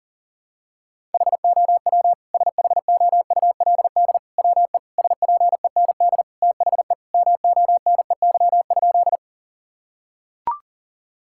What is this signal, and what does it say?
Morse code, heard as a keyed tone, about 40 wpm, 700 Hz. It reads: HOW SHOULD WE SPEND THE MONEY?
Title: Sentences from Top 1000 Words - Review Part 5 - 40wpm